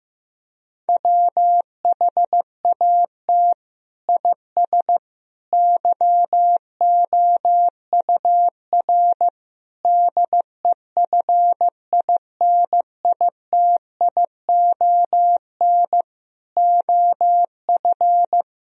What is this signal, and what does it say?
Morse code, heard as a keyed tone, about 14 wpm, 700 Hz. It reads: WHAT IS YOUR DEFINITION OF
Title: Sentences from Top 1000 Words - Review Part 4 - 15wpm